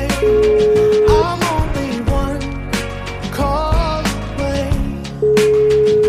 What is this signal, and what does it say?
Ringback tone (Europe, 425 Hz), with noise.